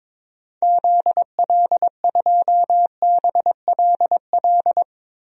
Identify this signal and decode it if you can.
Morse code, heard as a keyed tone, about 22 wpm, 700 Hz. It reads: ZL2BLL
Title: DXCC-100 Call Signs - 6 Characters - 22wpm